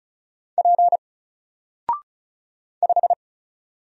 Morse code, heard as a keyed tone, about 36 wpm, 700 Hz. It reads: P 5
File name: Single-Letter-Number-35wpm